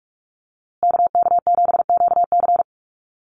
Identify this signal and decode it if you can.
Morse code, heard as a keyed tone, about 44 wpm, 700 Hz. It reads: KK7QC
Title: USA Call Signs Repeated 2x 45wpm